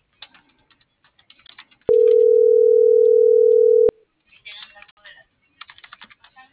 Ringback tone (North America), 440 + 480 Hz.